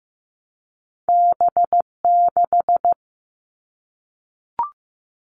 Morse code, heard as a keyed tone, about 15 wpm, 700 Hz. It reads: B6
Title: All Cut Numbers - 2-Digit 15wpm